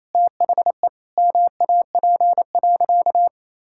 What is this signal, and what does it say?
Morse code, heard as a keyed tone, about 28 wpm, 700 Hz. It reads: THE MAP.
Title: Sentences from Top 1000 Words - Review Part 6 - 28wpm